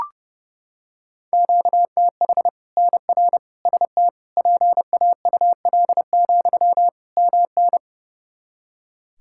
Morse code, heard as a keyed tone, about 30 wpm, 700 Hz. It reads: QTH NR ST PAUL, MN
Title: K1USN SST QSO Practice 30wpm